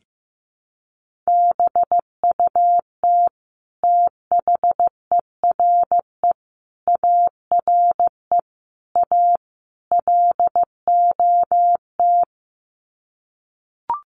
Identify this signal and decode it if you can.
Morse code, heard as a keyed tone, about 15 wpm, 700 Hz. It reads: BUT THERE ARE A LOT
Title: Sets of 5 Words Encore2 15wpm